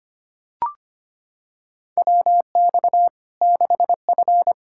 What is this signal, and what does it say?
Morse code, heard as a keyed tone, about 25 wpm, 700 Hz. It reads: WX 6F
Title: Practice QSO - Exchange 2 - Info - 25wpm